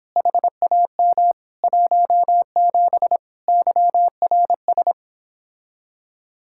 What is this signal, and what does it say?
Morse code, heard as a keyed tone, about 29 wpm, 700 Hz. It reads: HAM 17 YRS